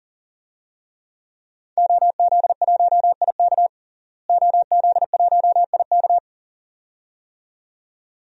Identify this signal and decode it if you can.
Morse code, heard as a keyed tone, about 40 wpm, 700 Hz. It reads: OZ1IK OZ1IK
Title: DXCC-100 Call Signs - 5 Characters - Repeated 2x - 40wpm